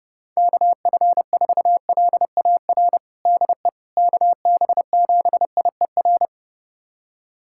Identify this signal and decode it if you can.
Morse code, heard as a keyed tone, about 31 wpm, 700 Hz. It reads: KF4LAR DE KB7IER
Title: USA Callsign DE Callsign - 6 Characters - 30wpm